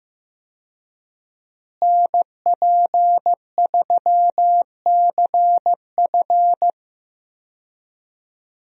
Morse code, heard as a keyed tone, about 16 wpm, 700 Hz. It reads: NP3CF
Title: USA Call Signs - 5 Characters - Repeated 2x - 15wpm